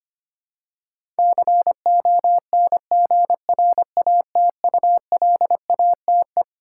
Morse code, heard as a keyed tone, about 25 wpm, 700 Hz. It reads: CONGRATULATE
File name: Twelve Letter Words - 25wpm